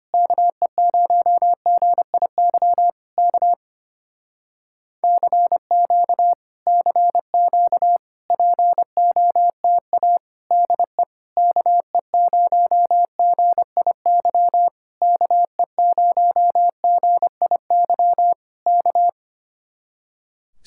Morse code, heard as a keyed tone, about 24 wpm, 700 Hz. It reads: KE0GIY K CQ CQ POTA DE KE0GIY KE0GIY K
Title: Speed-Racing Common QSO Phrases - 25wpm